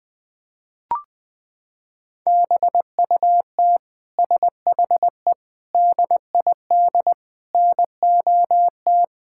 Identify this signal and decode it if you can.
Morse code, heard as a keyed tone, about 21 wpm, 700 Hz. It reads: BUT SHE DID NOT
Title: Sentences from Top 1600 Words 20wpm